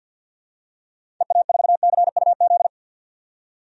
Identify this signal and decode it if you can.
Morse code, heard as a keyed tone, about 51 wpm, 700 Hz. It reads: EA4CUZ
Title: DXCC-100 Call Signs - 6 Characters - 50wpm